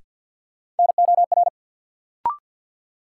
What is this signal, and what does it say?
Morse code, heard as a keyed tone, about 53 wpm, 700 Hz. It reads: NOR
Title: Sentences from Top 1500 Words - New Word Drill Alt - 50wpm